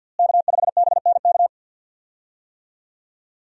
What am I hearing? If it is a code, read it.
Morse code, heard as a keyed tone, about 51 wpm, 700 Hz. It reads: K5BNK
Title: USA Call Signs - 5 Characters - Repeated 3x - 50wpm